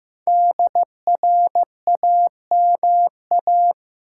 Morse code, heard as a keyed tone, about 16 wpm, 700 Hz. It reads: DRAMA
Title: Sentences from Top 1500 Words - New Word Drill - 15wpm